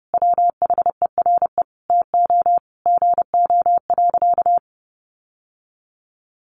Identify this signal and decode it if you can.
Morse code, heard as a keyed tone, about 30 wpm, 700 Hz. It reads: WHERE TO GO.
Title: Sentences from Top 700 Words 30wpm